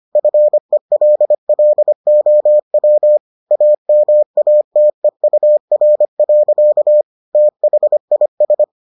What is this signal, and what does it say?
Morse code, heard as a keyed tone, about 25 wpm, 600 Hz. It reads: FELLOW AMATEUR. THIS